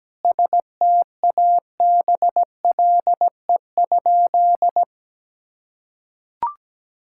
Morse code, heard as a keyed tone, about 17 wpm, 700 Hz. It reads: STABLE?
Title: Sentences from Top 1600 Words 17wpm